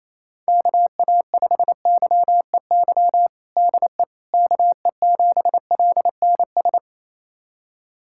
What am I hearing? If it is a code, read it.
Morse code, heard as a keyed tone, about 28 wpm, 700 Hz. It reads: KA5YEY DE KE7LNS
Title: USA Callsign DE Callsign 28wpm